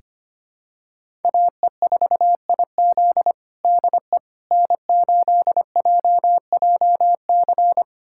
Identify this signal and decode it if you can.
Morse code, heard as a keyed tone, about 25 wpm, 700 Hz. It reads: AE4IZ DE N8JJC
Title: USA Callsign DE Callsign - 5 Characters - 25wpm